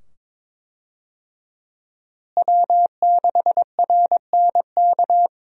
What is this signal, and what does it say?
Morse code, heard as a keyed tone, about 22 wpm, 700 Hz. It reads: W6RNK
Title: USA Call Signs - 5 Characters - Repeated 2x - 22wpm